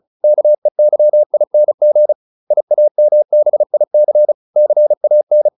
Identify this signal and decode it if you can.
Morse code, heard as a keyed tone, about 34 wpm, 600 Hz. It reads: KEYING IAMBIC CAN